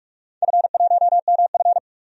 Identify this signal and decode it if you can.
Morse code, heard as a keyed tone, about 44 wpm, 690 Hz. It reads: F1MF
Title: DXCC-100 Call Signs - 4 Characters - 45wpm